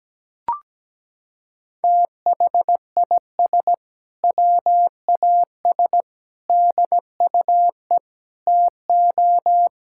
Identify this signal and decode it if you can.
Morse code, heard as a keyed tone, about 17 wpm, 700 Hz. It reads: THIS WAS DUE TO
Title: Sentences from Top 1600 Words 17wpm